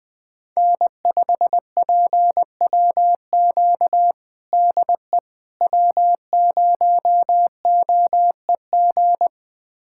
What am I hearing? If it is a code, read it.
Morse code, heard as a keyed tone, about 20 wpm, 700 Hz. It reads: N5PWQ DE W0OEG
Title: USA Callsign DE Callsign - 5 Characters - 20wpm